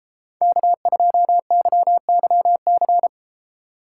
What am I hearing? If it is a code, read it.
Morse code, heard as a keyed tone, about 33 wpm, 700 Hz. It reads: K2YYC